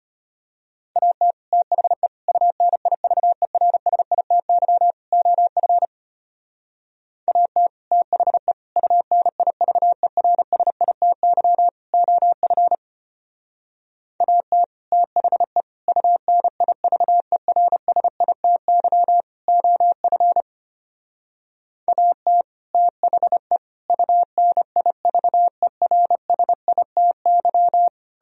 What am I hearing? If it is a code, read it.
Morse code, heard as a keyed tone, about 31 wpm, 700 Hz. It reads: AT THE UNIVERSITY OF AT THE UNIVERSITY OF AT THE UNIVERSITY OF AT THE UNIVERSITY
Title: Speed-Racing - Sets of 4 Words 25wpm